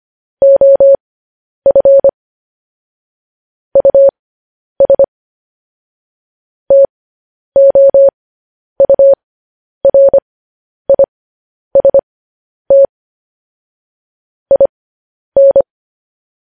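Morse code, heard as a keyed tone, about 25 wpm, spaced as for 10 wpm, 550 Hz. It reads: OF US TOURIST IN
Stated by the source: Morse code news headlines sent at 10 wpm.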